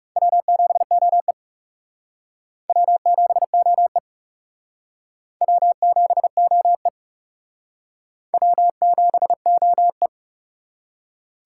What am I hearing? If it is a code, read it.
Morse code, heard as a keyed tone, about 39 wpm, 700 Hz. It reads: W7OE W7OE W7OE W7OE